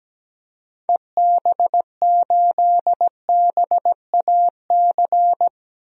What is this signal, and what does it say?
Morse code, heard as a keyed tone, about 17 wpm, 700 Hz. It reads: EB8BAC
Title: DXCC-100 Call Signs - 6 Characters - Repeated 2x - 17wpm